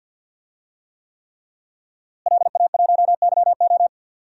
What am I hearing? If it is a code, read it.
Morse code, heard as a keyed tone, about 52 wpm, 700 Hz. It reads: LA1YO
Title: DXCC-100 Call Signs - 5 Characters - Repeated 3x - 50wpm